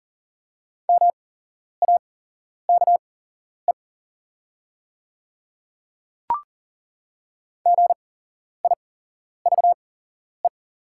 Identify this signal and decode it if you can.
Morse code, heard as a keyed tone, about 38 wpm, spaced as for 11 wpm, 700 Hz. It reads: MAKE GIVE
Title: 8x Character Spacing - Top 200 Words - 40wpm